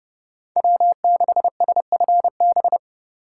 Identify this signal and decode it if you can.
Morse code, heard as a keyed tone, about 35 wpm, 700 Hz. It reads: W6SFB
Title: Speed Racing - USA 1x3 Call Signs 30wpm